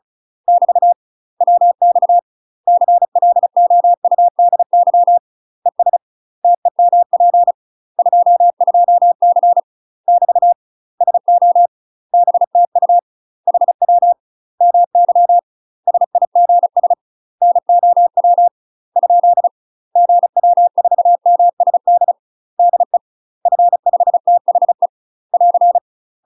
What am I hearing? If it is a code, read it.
Morse code, heard as a keyed tone, about 34 wpm, 700 Hz. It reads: <BT> WX CLOUDY ES TEMP 22C <BT> SO BTU HW MY SIGS NOW ? GW4MSD DE F5THE <AR>
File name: UK MASTER QSOs TRAINING / 35WPM / UK MASTER QSOs 0001